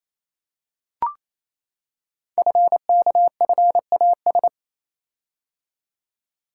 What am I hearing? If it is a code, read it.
Morse code, heard as a keyed tone, about 29 wpm, 700 Hz. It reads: FKFAS
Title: Code Groups - 5 Letter:Numbers - 28wpm